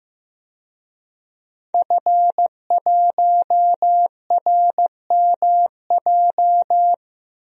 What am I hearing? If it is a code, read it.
Morse code, heard as a keyed tone, about 15 wpm, 700 Hz. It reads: F1RMJ
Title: DXCC-100 Call Signs - 5 Characters - 15wpm